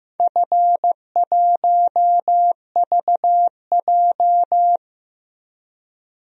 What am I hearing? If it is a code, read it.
Morse code, heard as a keyed tone, about 15 wpm, 700 Hz. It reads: F1VJ
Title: DXCC-100 Call Signs - 4 Characters - Repeated 3x - 15wpm